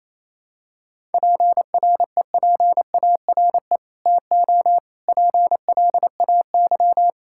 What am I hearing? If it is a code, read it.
Morse code, heard as a keyed tone, about 28 wpm, 700 Hz. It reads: PREPARE TO PLAY
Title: Sentences from Top 1000 Words - Review Part 6 - 28wpm